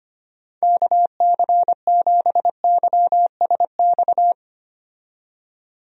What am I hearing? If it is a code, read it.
Morse code, heard as a keyed tone, about 25 wpm, 700 Hz. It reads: KC7YSX
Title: USA Call Signs - 6 Characters - 25wpm